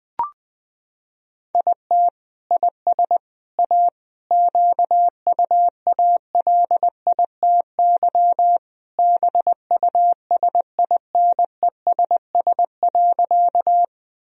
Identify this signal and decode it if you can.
Morse code, heard as a keyed tone, about 20 wpm, 700 Hz. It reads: IT IS A QUALITY BUSINESS.
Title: Sentences from Top 1000 Words - Review Part 4 - 20wpm